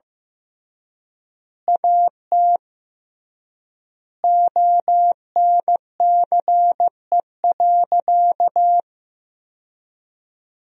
Morse code, heard as a keyed tone, about 15 wpm, 700 Hz. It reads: AT ONCE.
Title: 3x Word-Spacing - Sentences from Top 300 Words - 15wpm